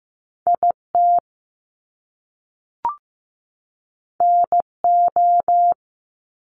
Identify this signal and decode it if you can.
Morse code, heard as a keyed tone, about 15 wpm, 700 Hz. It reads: IT NO
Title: Top Two Letter Combinations in Words 15wpm